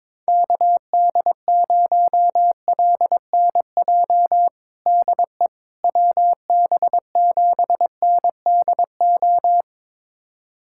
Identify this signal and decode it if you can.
Morse code, heard as a keyed tone, about 22 wpm, 700 Hz. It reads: KD0LNJ DE WB7NDO
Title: USA Callsign DE Callsign 22wpm